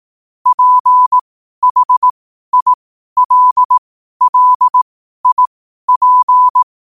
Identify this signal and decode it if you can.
Morse code, heard as a keyed tone, about 18 wpm, spaced as for 12 wpm, 1000 Hz. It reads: PHILLIP
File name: qotdfiles / 12wpm